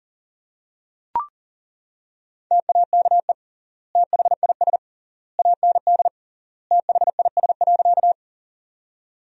Morse code, heard as a keyed tone, about 40 wpm, 700 Hz. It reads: TAKE THIS AND THIS.
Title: 3x Word-Spacing - Sentences from Top 200 Words - 40wpm